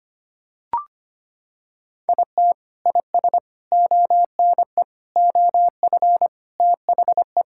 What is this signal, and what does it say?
Morse code, heard as a keyed tone, about 25 wpm, 700 Hz. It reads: IT IS ONE OF THE
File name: practice / 25wpm / Sentences with 4-Word Phrases - 25wpm